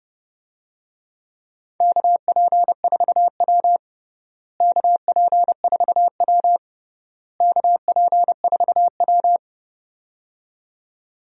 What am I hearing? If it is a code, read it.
Morse code, heard as a keyed tone, about 30 wpm, 700 Hz. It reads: KP4W KP4W KP4W
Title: DXCC-100 Call Signs - 4 Characters - Repeated 3x - 30wpm